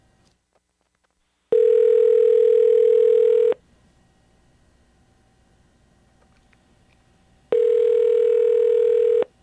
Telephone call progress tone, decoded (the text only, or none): ringback tone